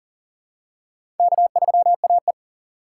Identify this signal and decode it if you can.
Morse code, heard as a keyed tone, about 40 wpm, 700 Hz. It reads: K3AE
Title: USA Call Signs - 4 Characters - 4x Character Spacing - 40wpm